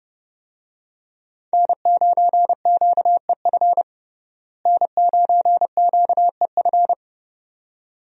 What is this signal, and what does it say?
Morse code, heard as a keyed tone, about 31 wpm, 700 Hz. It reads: N9QEF N9QEF
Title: USA Call Signs - 5 Characters - Repeated 2x - 30wpm